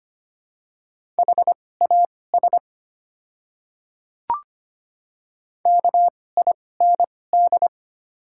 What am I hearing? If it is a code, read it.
Morse code, heard as a keyed tone, about 25 wpm, spaced as for 18 wpm, 700 Hz. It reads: HAS KIND
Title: 2x Character Spacing - Top 200 Words - 25wpm